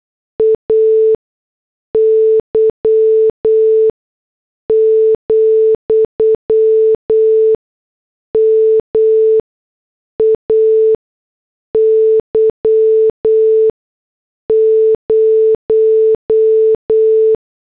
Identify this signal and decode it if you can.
Morse code, heard as a keyed tone, about 8 wpm, spaced as for 6 wpm, 440 Hz. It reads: AY, MAY 0
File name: morse / headlines